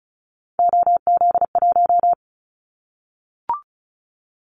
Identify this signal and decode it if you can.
Morse code, heard as a keyed tone, about 36 wpm, 700 Hz. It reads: OZ1
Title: Callsign Prefixes for Top 100 DXCC - 35wpm